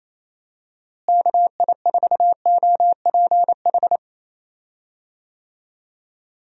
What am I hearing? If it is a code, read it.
Morse code, heard as a keyed tone, about 28 wpm, 700 Hz. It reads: KI4OPH
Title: USA Call Signs - 6 Characters - Repeated 2x - 28wpm